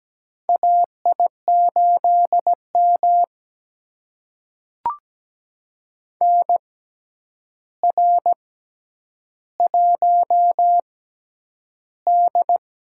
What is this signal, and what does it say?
Morse code, heard as a keyed tone, about 17 wpm, spaced as for 9 wpm, 700 Hz. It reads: AI8M NR1D
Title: USA Call Signs - 4 Characters - 6x Character Spacing - 17wpm